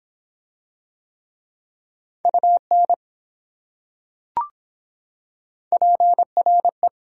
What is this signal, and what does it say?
Morse code, heard as a keyed tone, about 23 wpm, 700 Hz. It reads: UN PRE